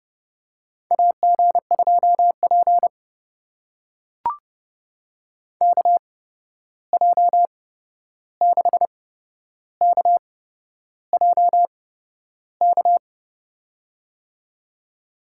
Morse code, heard as a keyed tone, about 30 wpm, spaced as for 11 wpm, 700 Hz. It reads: AG2P KJ6KJK
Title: USA Call Signs - 8x Character Spacing - 30wpm